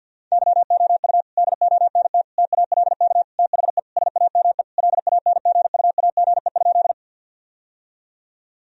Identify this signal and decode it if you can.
Morse code, heard as a keyed tone, about 52 wpm, 700 Hz. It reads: YOU DONT TALK THE SAME LANGUAGE?
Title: Sentences from Top 1000 Words - Review Part 3 - 50wpm